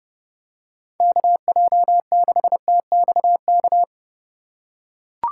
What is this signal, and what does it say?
Morse code, heard as a keyed tone, about 32 wpm, 700 Hz. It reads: KJ6TXK
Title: USA Call Signs - 6 Characters - 4x Character Spacing - 30wpm